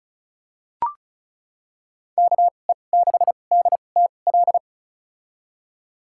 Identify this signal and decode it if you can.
Morse code, heard as a keyed tone, about 36 wpm, spaced as for 25 wpm, 700 Hz. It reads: KE6DTL
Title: USA Call Signs - 2x Character Spacing - 35wpm